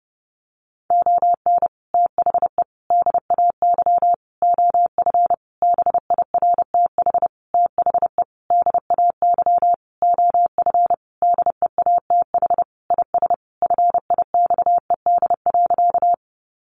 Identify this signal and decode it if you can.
Morse code, heard as a keyed tone, about 30 wpm, 700 Hz. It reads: ON THE DAY OF BIRTH THE DAY OF DEATH IS FIXED.
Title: Sentences from Top 1100 Words 30wpm